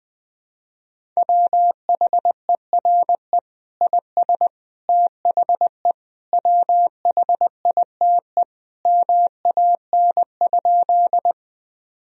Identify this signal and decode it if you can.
Morse code, heard as a keyed tone, about 21 wpm, 700 Hz. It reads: WHERE IS THE WHITE MAN?
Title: Sentences from Top 1000 Words - Review Part 1 - 20wpm